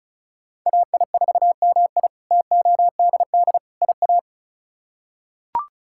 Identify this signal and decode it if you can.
Morse code, heard as a keyed tone, about 36 wpm, 700 Hz. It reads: AI4MI TODD IA
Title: K1USN SST QSO Practice - 35wpm